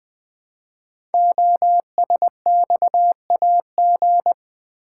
Morse code, heard as a keyed tone, about 20 wpm, 700 Hz. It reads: OSXAG
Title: Code Groups - 5 Letters - 20wpm